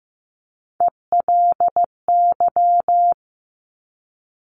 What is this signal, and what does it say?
Morse code, heard as a keyed tone, about 15 wpm, 700 Hz. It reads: ELY
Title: Top Three Letter Combinations in Words 15wpm